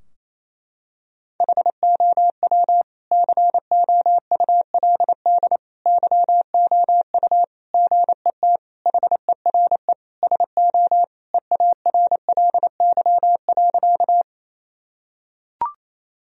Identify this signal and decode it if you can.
Morse code, heard as a keyed tone, about 27 wpm, 690 Hz. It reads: HOW COULD YOU GET HERE SO EARLY.
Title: Sentences from Top 1000 Words - Review Part 2 - 28wpm